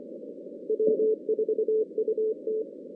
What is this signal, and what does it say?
Morse code, heard as a keyed tone, about 25 wpm, 410 Hz. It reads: W4UT
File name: 2016 JAN CW NAQP / 2016 Jan naqp k5ot - 0-8